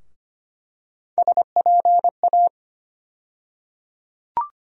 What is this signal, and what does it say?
Morse code, heard as a keyed tone, about 25 wpm, 700 Hz. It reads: SPA
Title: USA Call Sign Postfixes - 3 Letters 25wpm